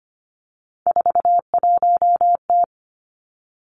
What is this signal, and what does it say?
Morse code, heard as a keyed tone, about 25 wpm, 700 Hz. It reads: 41T
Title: Three Challenging Letter-Number Combinations 25wpm